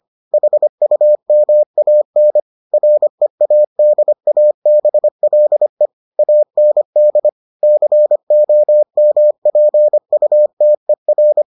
Morse code, heard as a keyed tone, about 25 wpm, 600 Hz. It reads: HUMAN READABLE AND COMPUTER